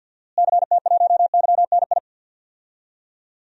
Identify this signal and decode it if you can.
Morse code, heard as a keyed tone, about 53 wpm, 700 Hz. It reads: CT1YNI
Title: DXCC-100 Call Signs - 6 Characters - Repeated 2x - 50wpm